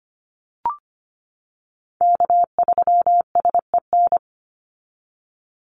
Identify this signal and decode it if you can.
Morse code, heard as a keyed tone, about 25 wpm, 700 Hz. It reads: K3SEN
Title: Call Signs - 25wpm